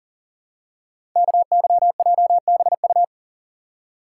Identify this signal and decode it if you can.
Morse code, heard as a keyed tone, about 40 wpm, 700 Hz. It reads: KYJBU
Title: Code Groups - 5 Letters - 40wpm